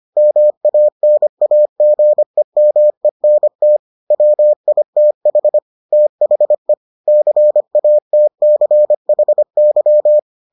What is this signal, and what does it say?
Morse code, heard as a keyed tone, about 25 wpm, 600 Hz. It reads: MANAGEMENT WITH THE CATCHY